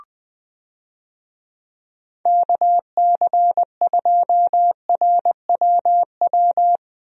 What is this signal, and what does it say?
Morse code, heard as a keyed tone, about 20 wpm, 700 Hz. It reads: KC2RWW
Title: USA Call Signs - 6 Characters - Repeated 2x - 20wpm